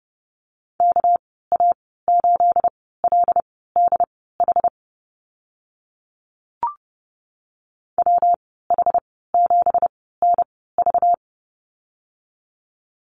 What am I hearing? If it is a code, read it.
Morse code, heard as a keyed tone, about 30 wpm, spaced as for 17 wpm, 700 Hz. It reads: KA8LDH WH7NV
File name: Lesson 37g - Letter x - Call Signs 30wpm